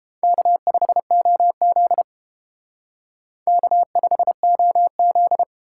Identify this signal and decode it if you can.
Morse code, heard as a keyed tone, about 28 wpm, 700 Hz. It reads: K5OZ K5OZ